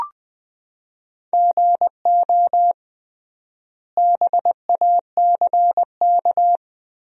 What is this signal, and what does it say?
Morse code, heard as a keyed tone, about 20 wpm, 700 Hz. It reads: GO BACK
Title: 3x Word-Spacing - Sentences from Top 300 Words - 20wpm